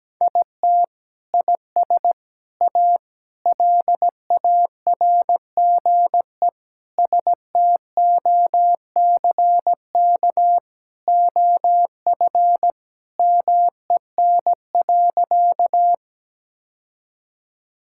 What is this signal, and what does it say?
Morse code, heard as a keyed tone, about 17 wpm, 700 Hz. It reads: IT IS A LARGE STOCK OF MEN.
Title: Sentences from Top 1000 Words - Review Part 5 - 17wpm